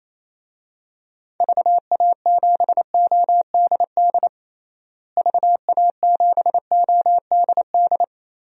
Morse code, heard as a keyed tone, about 28 wpm, 700 Hz. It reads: VA7ODD VA7ODD
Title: DXCC-100 Call Signs - 6 Characters - Repeated 3x - 28wpm